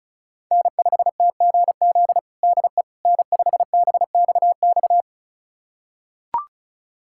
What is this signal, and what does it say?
Morse code, heard as a keyed tone, about 36 wpm, 700 Hz. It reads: N5TGZ DE N5BXX
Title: USA Callsign DE Callsign - 5 Characters - 35wpm